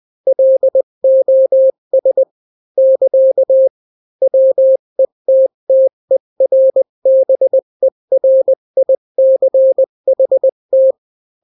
Morse code, heard as a keyed tone, about 20 wpm, 530 Hz. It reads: LOS <KA> WETTERBERICHT
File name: daily / 18wpm